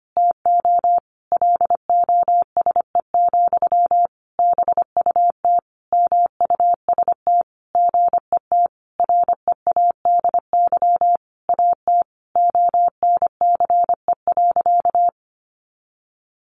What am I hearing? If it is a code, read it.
Morse code, heard as a keyed tone, about 25 wpm, 700 Hz. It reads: TO LOSE, BUT MUST GET READY AT ONCE.
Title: Sentences from Top 1100 Words 25wpm